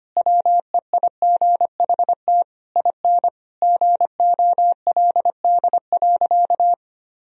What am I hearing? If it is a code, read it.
Morse code, heard as a keyed tone, about 25 wpm, 700 Hz. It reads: WEIGHT IN GOLD.
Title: Sentences from Top 1000 Words - Review Part 3 - 25wpm